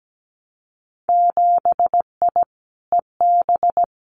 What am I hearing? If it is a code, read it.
Morse code, heard as a keyed tone, about 17 wpm, 700 Hz. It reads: 7I EB
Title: Winter Field Day Exchanges Encore 17wpm